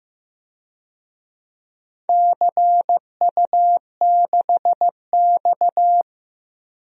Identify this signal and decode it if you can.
Morse code, heard as a keyed tone, about 15 wpm, 700 Hz. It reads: CU6X
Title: DXCC-100 Call Signs - 4 Characters - Repeated 3x - 15wpm